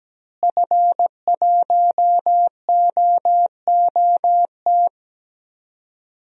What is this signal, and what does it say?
Morse code, heard as a keyed tone, about 17 wpm, 700 Hz. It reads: F1OOT
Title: DXCC-100 Call Signs - 5 Characters - Repeated 2x - 17wpm